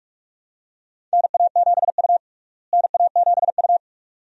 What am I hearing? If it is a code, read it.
Morse code, heard as a keyed tone, about 41 wpm, 700 Hz. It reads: NA7U NA7U
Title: USA Call Signs - 4 Characters - Repeated 2x - 45wpm